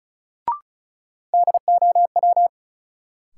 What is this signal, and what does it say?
Morse code, heard as a keyed tone, about 36 wpm, 700 Hz. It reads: DOW
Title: Sets of 5 Words - New Word Drill - 35wpm